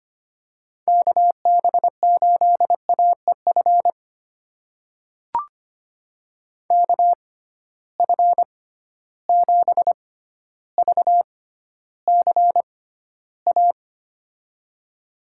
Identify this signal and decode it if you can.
Morse code, heard as a keyed tone, about 25 wpm, spaced as for 13 wpm, 700 Hz. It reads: KB8AEF KF7VCA
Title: USA Call Signs - 6 Characters - 6x Character Spacing - 25wpm